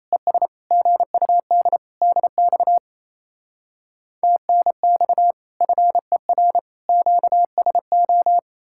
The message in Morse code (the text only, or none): ES GUD DX TNX FER QSO